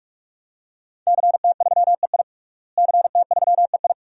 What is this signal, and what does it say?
Morse code, heard as a keyed tone, about 41 wpm, 700 Hz. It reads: CT3EI CT3EI
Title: DXCC-100 Call Signs - 5 Characters - Repeated 2x - 45wpm